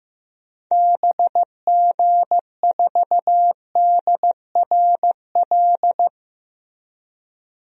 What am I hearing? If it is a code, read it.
Morse code, heard as a keyed tone, about 15 wpm, 700 Hz. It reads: BG4DRL
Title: DXCC-100 Call Signs - 6 Characters - Repeated 3x - 15wpm